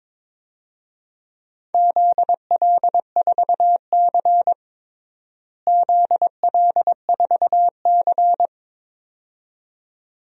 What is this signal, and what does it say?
Morse code, heard as a keyed tone, about 22 wpm, 700 Hz. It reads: ZL4C ZL4C
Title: DXCC-100 Call Signs - 4 Characters - Repeated 2x - 22wpm